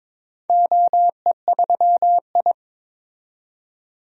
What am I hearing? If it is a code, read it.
Morse code, heard as a keyed tone, about 22 wpm, 700 Hz. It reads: OE3I
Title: DXCC-100 Call Signs - 4 Characters - 22wpm